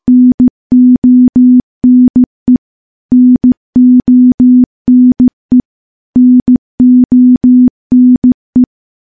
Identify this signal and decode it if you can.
Morse code, heard as a keyed tone, about 16 wpm, 250 Hz. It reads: NONE NONE NONE
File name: morse / morse-training-15WPM